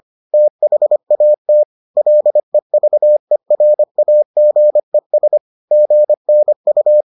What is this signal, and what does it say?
Morse code, heard as a keyed tone, about 25 wpm, 600 Hz. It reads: THAT LEVERAGES GNU